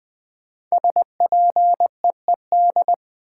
Morse code, heard as a keyed tone, about 20 wpm, 700 Hz. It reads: SPEED
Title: Sentences from Top 800 Words - New Word Drill - 20wpm